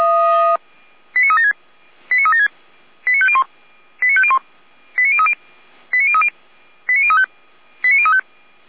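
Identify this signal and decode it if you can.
Five-tone selective call (ZVEI), heard as quick runs of five tones, tones of about 70 ms. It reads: 89376 89376 89501 89501 89030 89030 89035 89035
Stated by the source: Five-tone ZVEI selective call received by SDR, from audio server of Feuerwehr Orlamünde.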